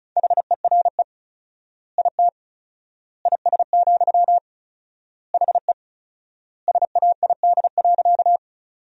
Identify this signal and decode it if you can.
Morse code, heard as a keyed tone, about 36 wpm, 700 Hz. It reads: HERE IT IS, HE SAID.
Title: 4x Word-Spacing - Sentences from Top 200 Words - 35wpm